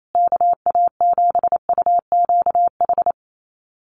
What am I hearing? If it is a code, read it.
Morse code, heard as a keyed tone, about 28 wpm, 700 Hz. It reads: KA7UQH
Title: USA Call Signs Repeated 2x 28wpm